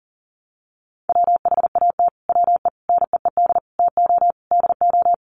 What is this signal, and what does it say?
Morse code, heard as a keyed tone, about 40 wpm, 700 Hz. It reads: WHAT WE NEED TO DO
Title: Sets of 5 Words Encore 40wpm